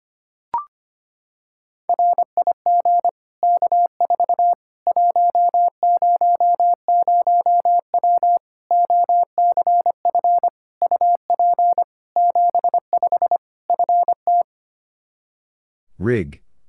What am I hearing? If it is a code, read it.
Morse code, heard as a keyed tone, about 25 wpm, 700 Hz. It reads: RIG K4 100W OCF UP 75 FT
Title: Practice QSO - Exchange 2 - Info - 25wpm